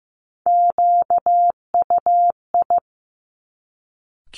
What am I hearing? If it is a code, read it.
Morse code, heard as a keyed tone, about 15 wpm, 700 Hz. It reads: QUI
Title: Top Three Letter Combinations in Words 15wpm